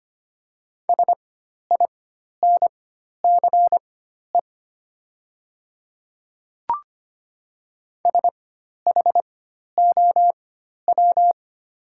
Morse code, heard as a keyed tone, about 25 wpm, spaced as for 12 wpm, 700 Hz. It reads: SINCE SHOW
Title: 4x Character Spacing - Top 300 Words - 25wpm